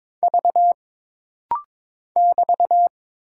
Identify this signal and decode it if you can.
Morse code, heard as a keyed tone, about 22 wpm, 700 Hz. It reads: V <BT>
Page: This rapid-fire timing is geared to push your ICR (Instant Character Recognition) skills.